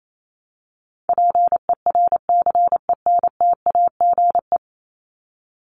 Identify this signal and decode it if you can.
Morse code, heard as a keyed tone, about 28 wpm, 700 Hz. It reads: PERCENTAGE
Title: Top 1001-1250 Words 28wpm